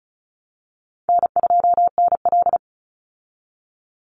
Morse code, heard as a keyed tone, about 34 wpm, 700 Hz. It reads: N2NL
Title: Easy Call Signs Repeated 2x - 35wpm